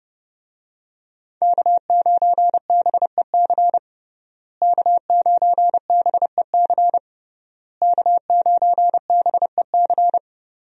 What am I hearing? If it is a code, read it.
Morse code, heard as a keyed tone, about 30 wpm, 700 Hz. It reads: K9BEC K9BEC K9BEC
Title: USA Call Signs - 5 Characters - Repeated 3x - 30wpm